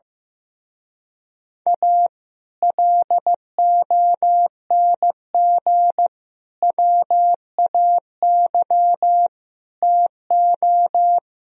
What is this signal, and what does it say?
Morse code, heard as a keyed tone, about 15 wpm, 700 Hz. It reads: A LONG WAY TO
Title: Speed-Racing - Sets of 5 Words 15wpm